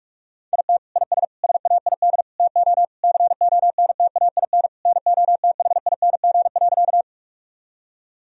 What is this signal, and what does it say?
Morse code, heard as a keyed tone, about 41 wpm, 700 Hz. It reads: IT IS SAID TO CONTAIN NOTHING.
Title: Sentences from Top 1000 Words - Review Part 2 - 45wpm